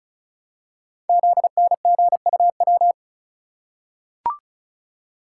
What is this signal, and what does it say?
Morse code, heard as a keyed tone, about 36 wpm, 700 Hz. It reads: ZNGUW
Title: Code Groups - 5 Letters - 35wpm